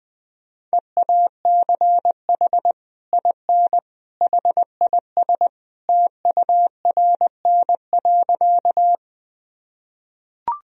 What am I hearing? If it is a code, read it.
Morse code, heard as a keyed tone, about 20 wpm, 700 Hz. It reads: EACH IN HIS TURN.
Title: Sentences from Top 1000 Words - Review Part 1 - 20wpm